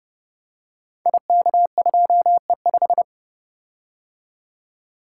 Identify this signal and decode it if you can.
Morse code, heard as a keyed tone, about 30 wpm, 700 Hz. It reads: IK2E5
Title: Code Groups - 5 Letter:Numbers - 30wpm